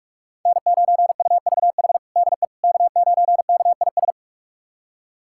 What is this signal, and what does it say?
Morse code, heard as a keyed tone, about 46 wpm, 700 Hz. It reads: N9UVH DE K9KIS
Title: USA Callsign DE Callsign - 5 Characters - 45wpm